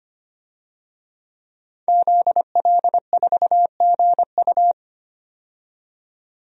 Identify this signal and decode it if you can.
Morse code, heard as a keyed tone, about 25 wpm, 700 Hz. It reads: ZL4GU
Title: DXCC-100 Call Signs - 5 Characters - 25wpm